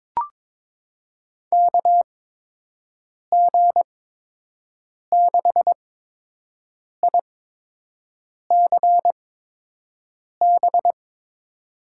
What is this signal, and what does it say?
Morse code, heard as a keyed tone, about 23 wpm, spaced as for 6 wpm, 700 Hz. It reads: KG6ICB
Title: USA Call Signs - 6 Characters - 8x Character Spacing - 22wpm